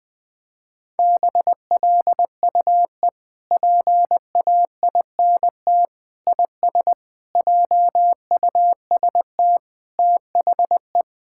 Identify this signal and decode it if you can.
Morse code, heard as a keyed tone, about 21 wpm, 700 Hz. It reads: BLUE PAINT IS JUST THE
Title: Sentences from Top 1000 Words - Review Part 6 - 20wpm